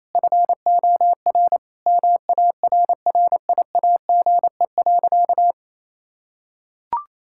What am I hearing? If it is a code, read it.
Morse code, heard as a keyed tone, about 28 wpm, 700 Hz. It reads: FOR MARRIAGE.
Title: Sentences from Top 1000 Words - Review Part 6 - 28wpm